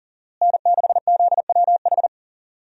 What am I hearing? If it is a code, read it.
Morse code, heard as a keyed tone, about 39 wpm, 700 Hz. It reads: N6ZWH
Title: USA Call Signs - 5 Characters - Repeated 2x - 40wpm